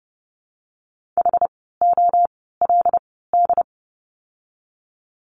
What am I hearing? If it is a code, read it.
Morse code, heard as a keyed tone, about 30 wpm, spaced as for 17 wpm, 700 Hz. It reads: HOLD
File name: Lesson 17f - Letter u - Words 30wpm